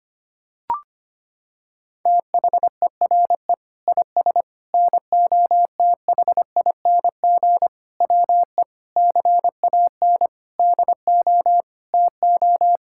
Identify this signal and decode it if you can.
Morse code, heard as a keyed tone, about 25 wpm, 700 Hz. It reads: THERE IS NOTHING WE CAN DO TO
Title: Sentences from Top 1000 Words - Review Part 6 - 25wpm